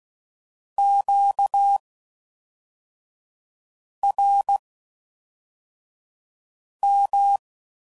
Morse code, heard as a keyed tone, about 16 wpm, spaced as for 4 wpm, 800 Hz. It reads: QRM